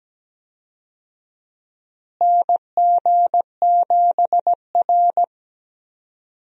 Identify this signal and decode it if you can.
Morse code, heard as a keyed tone, about 17 wpm, 700 Hz. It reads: NG7R
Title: USA Call Signs - 4 Characters - Repeated 2x - 17wpm